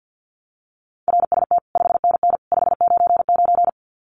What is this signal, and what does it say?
Morse code, heard as a keyed tone, about 50 wpm, 700 Hz. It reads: RST 5NN 599
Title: Common QSO Phrases 50wpm